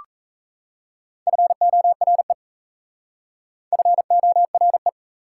Morse code, heard as a keyed tone, about 38 wpm, 690 Hz. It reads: FORE FORE